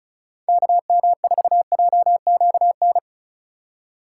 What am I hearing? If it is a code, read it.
Morse code, heard as a keyed tone, about 43 wpm, 700 Hz. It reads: KM4JQN
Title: Speed Racing - USA 2x3 Call Signs 35wpm